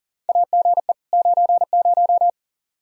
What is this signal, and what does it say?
Morse code, heard as a keyed tone, about 40 wpm, 700 Hz. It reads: AGE 90
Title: K1USN SST QSO Practice 40wpm